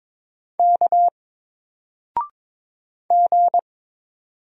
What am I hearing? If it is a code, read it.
Morse code, heard as a keyed tone, about 22 wpm, 700 Hz. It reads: K G
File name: Single-Letter-Number-22wpm